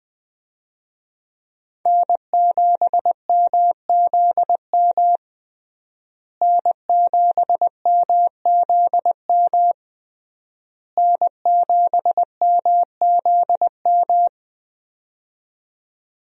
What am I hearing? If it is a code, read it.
Morse code, heard as a keyed tone, about 20 wpm, 700 Hz. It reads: N7MZM N7MZM N7MZM
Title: USA Call Signs - 5 Characters - Repeated 3x - 20wpm